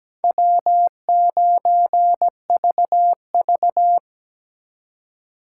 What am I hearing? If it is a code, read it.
Morse code, heard as a keyed tone, about 20 wpm, 700 Hz. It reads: W9VV
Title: Speed Racing - USA Call Signs 17wpm